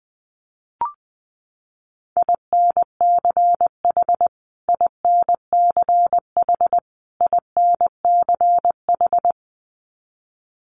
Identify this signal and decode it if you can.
Morse code, heard as a keyed tone, about 20 wpm, 700 Hz. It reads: INCH INCH INCH
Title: Top 500 Words Repeated 20wpm